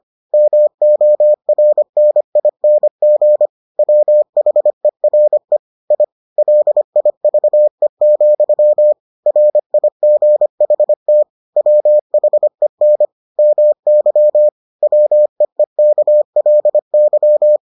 Morse code, heard as a keyed tone, about 25 wpm, 600 Hz. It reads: MORNING WHERE I LIVE, RIGHT WHEN MY WEEKLY